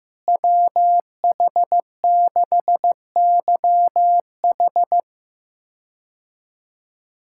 Morse code, heard as a keyed tone, about 15 wpm, 700 Hz. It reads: WH6YH
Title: K1USN SST QSO Practice - 15wpm